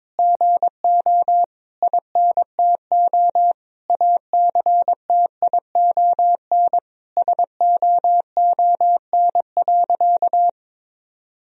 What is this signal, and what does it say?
Morse code, heard as a keyed tone, about 22 wpm, 700 Hz. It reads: GO INTO ACTION SOON.
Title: Sentences from Top 1000 Words - Review Part 4 - 22wpm